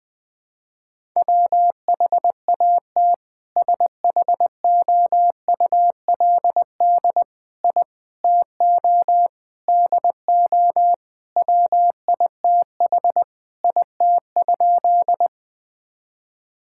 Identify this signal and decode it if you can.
Morse code, heard as a keyed tone, about 20 wpm, 700 Hz. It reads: WHAT SHOULD I TO DO WITH IT?
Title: Sentences with 4-Word Phrases - 20wpm